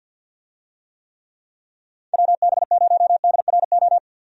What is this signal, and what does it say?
Morse code, heard as a keyed tone, about 47 wpm, 700 Hz. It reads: WB0DRO
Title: USA Call Signs - 6 Characters - 50wpm